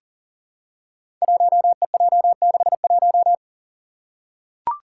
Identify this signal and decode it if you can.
Morse code, heard as a keyed tone, about 40 wpm, 700 Hz. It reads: 1EJ61
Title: Five Challenging Letter-Number Combinations 40wpm